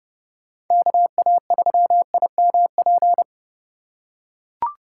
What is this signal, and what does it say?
Morse code, heard as a keyed tone, about 29 wpm, 700 Hz. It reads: KA3IMP
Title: USA Call Signs - 6 Characters - 6x Character Spacing - 30wpm